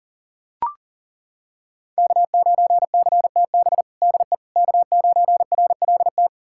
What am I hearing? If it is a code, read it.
Morse code, heard as a keyed tone, about 40 wpm, 700 Hz. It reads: K9CTB DE K9RLT
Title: USA Callsign DE Callsign - 5 Characters - 40wpm